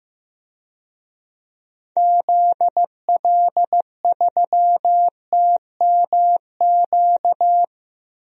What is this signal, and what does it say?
Morse code, heard as a keyed tone, about 15 wpm, 700 Hz. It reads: ZL3TMQ
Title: DXCC-100 Call Signs - 6 Characters - 15wpm